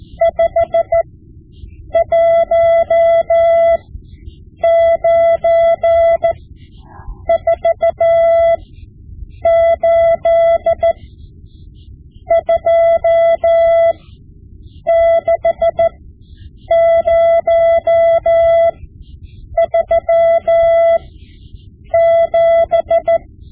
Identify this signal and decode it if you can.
Morse code, heard as a keyed tone, about 14 wpm, spaced as for 7 wpm, 670 Hz. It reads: 51948 26037
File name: Morse Lessons / Morse Lessons / Tape-5B